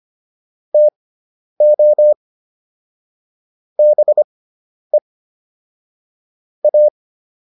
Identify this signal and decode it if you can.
Morse code, heard as a keyed tone, about 25 wpm, spaced as for 10 wpm, 600 Hz. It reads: TO BE A
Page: intermediate-practice-files